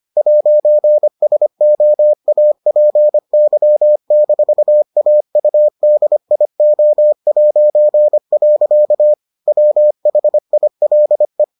Morse code, heard as a keyed tone, about 25 wpm, 600 Hz. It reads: 'SOAPY-AUDIO'. WHILE